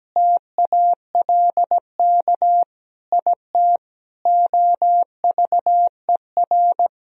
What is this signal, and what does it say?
Morse code, heard as a keyed tone, about 17 wpm, 700 Hz. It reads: TALK IT OVER
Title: Sentences from Top 1000 Words - Review Part 2 - 17wpm